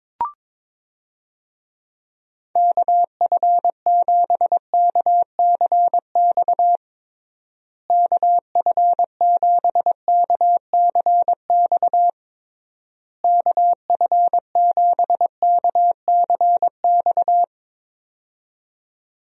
Morse code, heard as a keyed tone, about 22 wpm, 700 Hz. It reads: KF7KCX KF7KCX KF7KCX
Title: USA Call Signs - 6 Characters - Repeated 3x - 22wpm